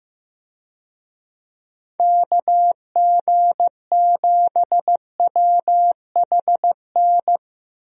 Morse code, heard as a keyed tone, about 15 wpm, 700 Hz. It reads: KG7WHN
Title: USA Call Signs - 6 Characters - Repeated 2x - 15wpm